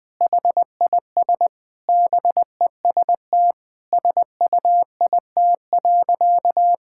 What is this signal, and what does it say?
Morse code, heard as a keyed tone, about 21 wpm, 700 Hz. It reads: HIS BEST SUIT.
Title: Sentences from Top 1600 Words 20wpm